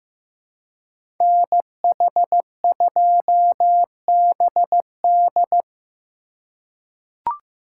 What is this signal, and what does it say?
Morse code, heard as a keyed tone, about 15 wpm, 700 Hz. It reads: NH2BD
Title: USA Call Signs - 5 Characters - 2x Character Spacing - 15wpm